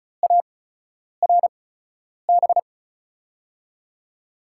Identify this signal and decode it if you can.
Morse code, heard as a keyed tone, about 36 wpm, spaced as for 10 wpm, 700 Hz. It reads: ARB
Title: USA Call Signs - 8x Character Spacing - 35wpm